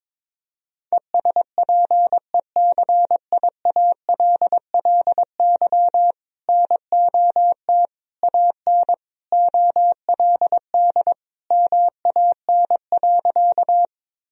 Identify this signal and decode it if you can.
Morse code, heard as a keyed tone, about 22 wpm, 700 Hz. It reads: ESPECIALLY NOT AN OLD MAN.
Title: Sentences from Top 1000 Words - Review Part 4 - 22wpm